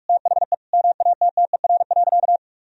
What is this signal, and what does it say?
Morse code, heard as a keyed tone, about 46 wpm, 700 Hz. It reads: THE MATTER.
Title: Sentences from Top 1000 Words - Review Part 4 - 45wpm